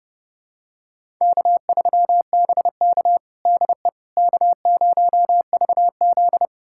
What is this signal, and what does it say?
Morse code, heard as a keyed tone, about 30 wpm, 700 Hz. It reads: K3BK DE K0VZ
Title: USA Callsign DE Callsign - 4 Characters - 30wpm